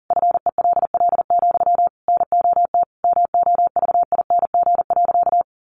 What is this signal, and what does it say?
Morse code, heard as a keyed tone, about 40 wpm, 700 Hz. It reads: FELL, NOT MOVING.
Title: A Sound Of Thunder 40wpm